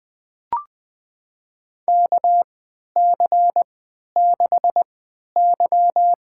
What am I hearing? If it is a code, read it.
Morse code, heard as a keyed tone, about 20 wpm, spaced as for 11 wpm, 700 Hz. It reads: KC6Y
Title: USA Call Signs - 3x Character Spacing - 20wpm